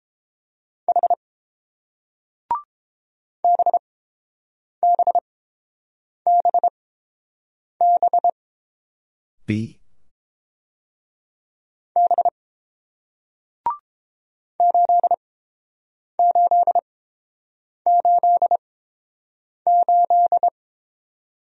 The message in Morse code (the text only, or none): H BBBB B 8888